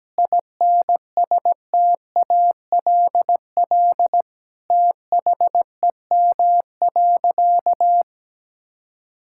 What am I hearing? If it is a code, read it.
Morse code, heard as a keyed tone, about 17 wpm, 700 Hz. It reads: INSTALL THEM.
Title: Sentences from Top 1600 Words 17wpm